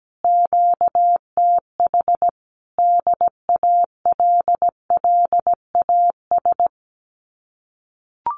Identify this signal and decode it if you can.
Morse code, heard as a keyed tone, about 17 wpm, 700 Hz. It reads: QTH DALLAS
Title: Common QSO Phrases Encore 17wpm